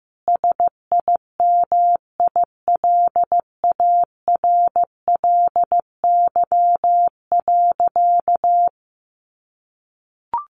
Morse code, heard as a keyed tone, about 15 wpm, 700 Hz. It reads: SIMILARLY.
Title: Sentences from Top 1400 Words 15wpm